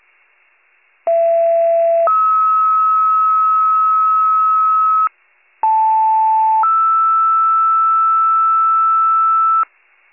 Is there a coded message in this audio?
Yes: two-tone page from 1.1 to 9.6 s, heard as a one-second tone, then a three-second tone, about 670 then 1290 Hz, about 880 then 1320 Hz.